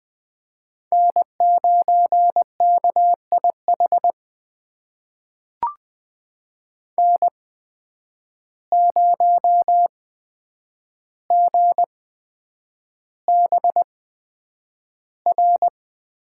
Morse code, heard as a keyed tone, about 20 wpm, spaced as for 9 wpm, 700 Hz. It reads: N9KIH N0GBR
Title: USA Call Signs - 5 Characters - 8x Character Spacing - 20wpm